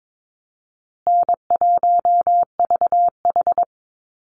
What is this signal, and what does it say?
Morse code, heard as a keyed tone, about 22 wpm, 700 Hz. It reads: N1VH
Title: Call Signs Repeated 2x - 22wpm